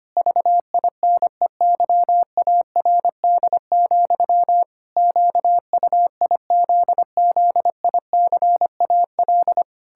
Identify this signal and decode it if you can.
Morse code, heard as a keyed tone, about 25 wpm, 700 Hz. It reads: VINEYARD, QUIZZICAL
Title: Pangram Challenge v2 - 25wpm